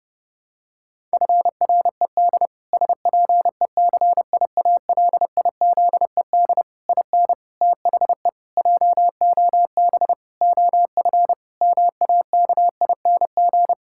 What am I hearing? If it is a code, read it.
Morse code, heard as a keyed tone, about 30 wpm, 700 Hz. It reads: FRED SPECIALIZED IN THE JOB OF MAKING
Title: Pangram Challenge v2 - 30wpm